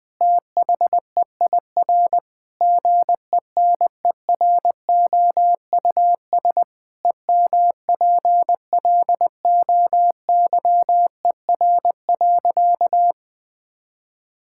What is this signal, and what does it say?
Morse code, heard as a keyed tone, about 20 wpm, 700 Hz. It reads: THEIR GENEROUS EMPLOYER.
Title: Pangram Challenge v2 - 20wpm